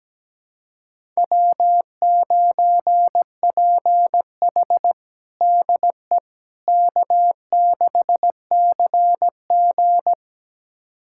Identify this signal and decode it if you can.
Morse code, heard as a keyed tone, about 17 wpm, 700 Hz. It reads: W9PH DE K6CG
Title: USA Callsign DE Callsign - 4 Characters - 17wpm